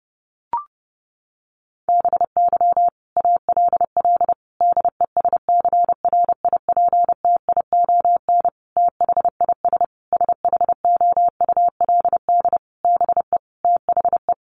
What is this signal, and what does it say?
Morse code, heard as a keyed tone, about 30 wpm, 700 Hz. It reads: BY ALL DESCRIPTION THIS SHOULD BE THE
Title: Sentences from Top 1300 Words 30wpm